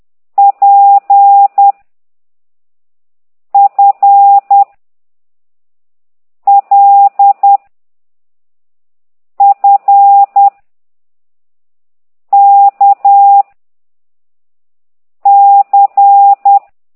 Morse code, heard as a keyed tone, about 10 wpm, 800 Hz. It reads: P F L F K C